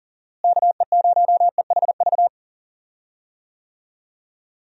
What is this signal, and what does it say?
Morse code, heard as a keyed tone, about 37 wpm, 700 Hz. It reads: KE0EHV
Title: USA Call Signs - 6 Characters - Repeated 2x - 40wpm